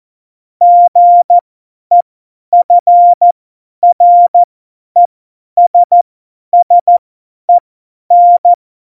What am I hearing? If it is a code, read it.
Morse code, heard as a keyed tone, about 14 wpm, spaced as for 10 wpm, 700 Hz. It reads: GEFRESSEN
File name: Morsekurs / Witze / Witz 17.Juli 2018, 50 F70